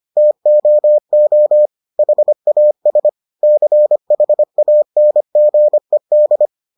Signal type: Morse code

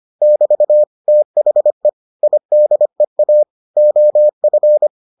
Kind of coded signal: Morse code

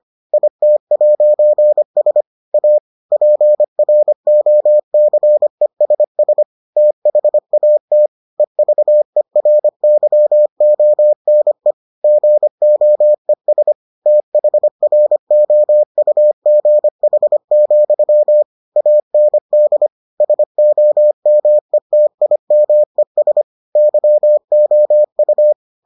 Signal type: Morse code

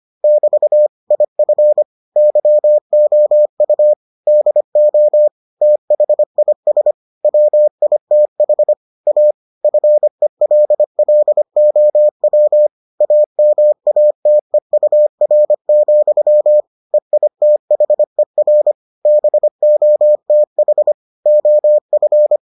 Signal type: Morse code

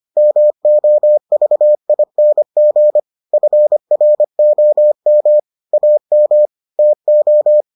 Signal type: Morse code

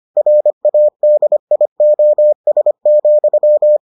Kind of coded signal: Morse code

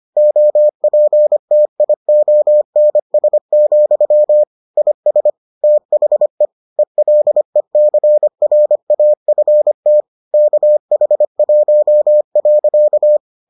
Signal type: Morse code

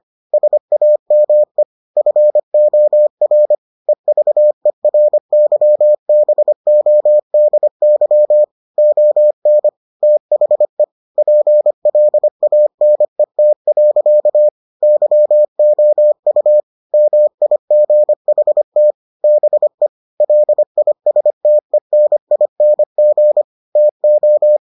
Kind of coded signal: Morse code

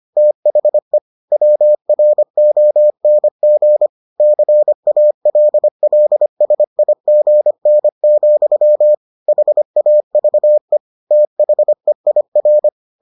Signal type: Morse code

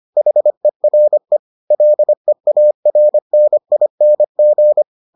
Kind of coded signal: Morse code

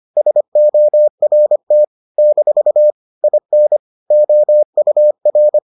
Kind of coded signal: Morse code

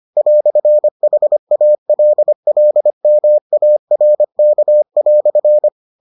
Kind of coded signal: Morse code

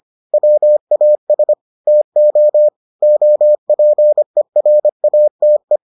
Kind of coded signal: Morse code

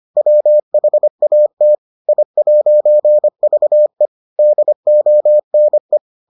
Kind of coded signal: Morse code

